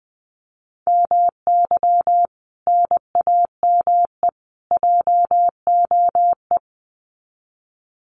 Morse code, heard as a keyed tone, about 20 wpm, 700 Hz. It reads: MY NAME JOE
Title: Common QSO Phrases 20wpm